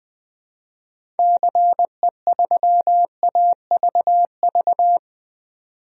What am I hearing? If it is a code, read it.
Morse code, heard as a keyed tone, about 20 wpm, 700 Hz. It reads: CE3AVV
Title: DXCC-100 Call Signs - 6 Characters - 20wpm